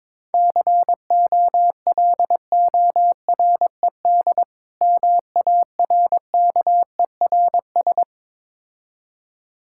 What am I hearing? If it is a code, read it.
Morse code, heard as a keyed tone, about 21 wpm, 690 Hz. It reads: COLORED MARKERS
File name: Difficult Words with Hints - 22wpm